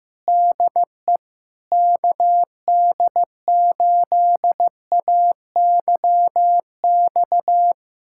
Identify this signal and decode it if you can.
Morse code, heard as a keyed tone, about 15 wpm, 700 Hz. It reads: DE KD8AYX
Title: USA Callsign DE Callsign 15wpm